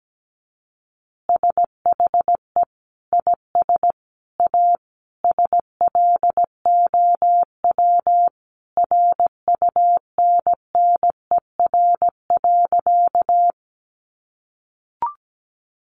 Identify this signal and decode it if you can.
Morse code, heard as a keyed tone, about 17 wpm, 700 Hz. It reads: SHE IS A SLOW RUNNER.
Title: Additional Easy Phrases 17wpm